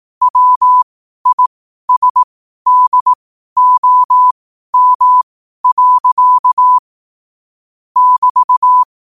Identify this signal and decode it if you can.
Morse code, heard as a keyed tone, about 18 wpm, spaced as for 12 wpm, 1000 Hz. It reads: WISDOM. <BT>
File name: qotdfiles / 12wpm